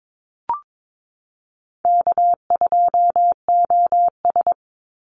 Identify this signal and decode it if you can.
Morse code, heard as a keyed tone, about 22 wpm, 700 Hz. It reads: K2OS
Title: Easy Call Signs Repeated 2x - 22wpm